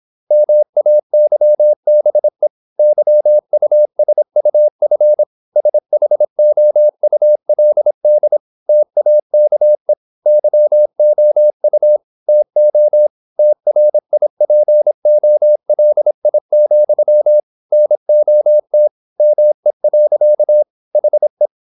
Morse code, heard as a keyed tone, about 26 wpm, 600 Hz. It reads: MAYBE YUSUF SHOULD TAKE YOU TO TRIPOLI, NOT ME. HE
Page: Never 26wpm | CW med Gnister